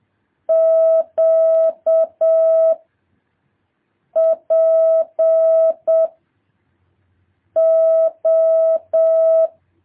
Morse code, heard as a keyed tone, about 7 wpm, 650 Hz. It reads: Q P O
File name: jh8cbh / 2023cw